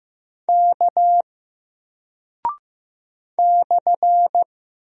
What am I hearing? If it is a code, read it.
Morse code, heard as a keyed tone, about 15 wpm, 700 Hz. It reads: K /
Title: Single Character (Letter:Number:Punctuation:Prosign) 15wpm